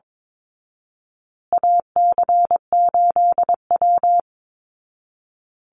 Morse code, heard as a keyed tone, about 22 wpm, 700 Hz. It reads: AC8W
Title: Call Signs Repeated 2x - 22wpm